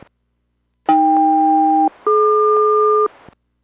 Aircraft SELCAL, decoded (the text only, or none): AK-DQ